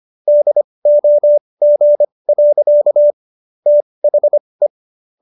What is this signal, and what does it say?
Morse code, heard as a keyed tone, about 25 wpm, spaced as for 20 wpm, 590 Hz. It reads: DOG. THE